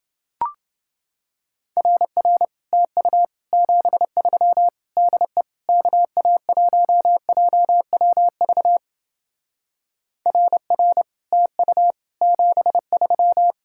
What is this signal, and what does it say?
Morse code, heard as a keyed tone, about 28 wpm, 700 Hz. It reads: RR TU 73 DE KA1JWV RR TU 73